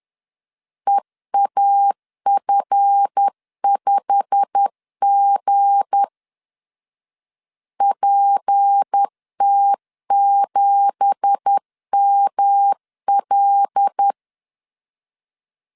Morse code, heard as a keyed tone, about 10 wpm, 800 Hz. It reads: EAF5G PT7ML
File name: CW / DET / 1 9 1500C S B 10wpm 4439